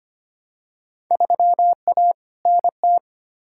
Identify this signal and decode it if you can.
Morse code, heard as a keyed tone, about 25 wpm, 700 Hz. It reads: 3A NT
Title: Field Day Exchanges Encore - 25wpm